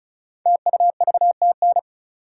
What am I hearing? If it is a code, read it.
Morse code, heard as a keyed tone, about 36 wpm, 700 Hz. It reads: TUVTN
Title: Code Groups - 5 Letters - 35wpm